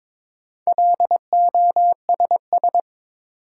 Morse code, heard as a keyed tone, about 22 wpm, 700 Hz. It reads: LOSS
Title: Sentences from Top 1400 Words - New Word Drill Alt - 22wpm